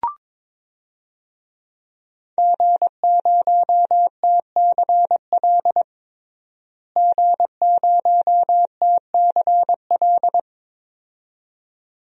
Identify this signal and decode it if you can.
Morse code, heard as a keyed tone, about 23 wpm, 700 Hz. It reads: G0TCL G0TCL
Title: DXCC-100 Call Signs - 5 Characters - Repeated 2x - 22wpm